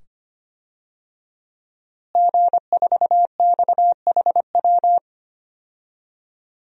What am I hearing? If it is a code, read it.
Morse code, heard as a keyed tone, about 25 wpm, 700 Hz. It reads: G4XHW
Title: DXCC-100 Call Signs - 5 Characters - Repeated 2x - 25wpm